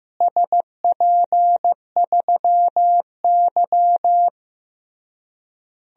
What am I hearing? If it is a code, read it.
Morse code, heard as a keyed tone, about 15 wpm, 700 Hz. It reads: SP3Y
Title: DXCC-100 Call Signs - 4 Characters - Repeated 3x - 15wpm